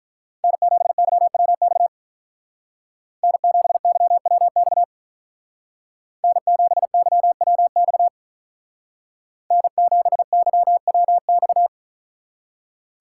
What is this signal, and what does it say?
Morse code, heard as a keyed tone, about 44 wpm, 700 Hz. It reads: N7YWX N7YWX N7YWX N7YWX